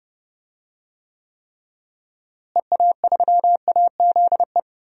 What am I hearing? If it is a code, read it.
Morse code, heard as a keyed tone, about 30 wpm, 700 Hz. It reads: EA3AZE
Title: DXCC-100 Call Signs - 6 Characters - Repeated 3x - 30wpm